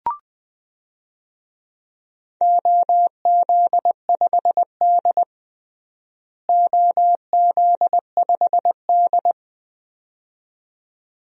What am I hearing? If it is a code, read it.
Morse code, heard as a keyed tone, about 20 wpm, 700 Hz. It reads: OZ5D OZ5D
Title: DXCC-100 Call Signs - 4 Characters - Repeated 2x - 20wpm